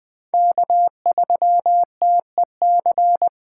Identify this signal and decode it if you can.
Morse code, heard as a keyed tone, about 20 wpm, 700 Hz. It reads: K3TEC
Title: Speed Racing - USA 1x3 Call Signs 17wpm